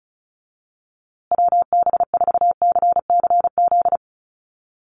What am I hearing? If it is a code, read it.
Morse code, heard as a keyed tone, about 34 wpm, 700 Hz. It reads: WB4CCZ
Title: USA Call Signs Repeated 2x 35wpm